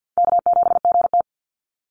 Morse code, heard as a keyed tone, about 49 wpm, 700 Hz. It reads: K7GT
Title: NA SOTA Chaser Callsigns 50wpm